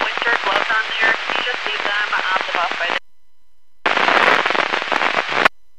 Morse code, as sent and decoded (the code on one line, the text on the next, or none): none